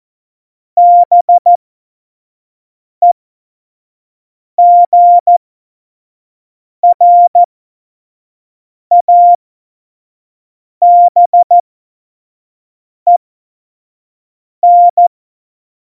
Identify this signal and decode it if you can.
Morse code, heard as a keyed tone, about 14 wpm, spaced as for 5 wpm, 700 Hz. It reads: BEGRABEN
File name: Morsekurs / Witze / Witz 15. Mai 2018, 25 F70